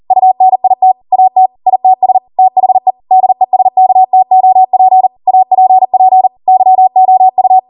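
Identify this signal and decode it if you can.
Morse code, heard as a keyed tone, about 40 wpm, 750 Hz. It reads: UNIT AT ITS THE DESKTOP APP YOU
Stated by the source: Code practice at 40 wpm.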